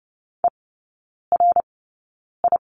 Morse code, heard as a keyed tone, about 30 wpm, spaced as for 9 wpm, 700 Hz. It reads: ERI
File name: 081 Lesson 14e - Number 2 - Three Characters 30wpm